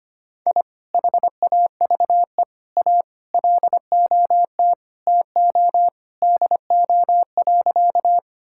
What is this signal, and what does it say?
Morse code, heard as a keyed tone, about 25 wpm, 700 Hz. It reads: I HAVE A LOT TO DO.
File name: practice / 25wpm / Sentences with 2-Word Phrases - 25wpm